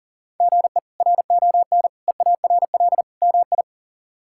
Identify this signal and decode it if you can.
Morse code, heard as a keyed tone, about 40 wpm, 700 Hz. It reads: GE RON EARL MI
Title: K1USN SST QSO Practice - 40wpm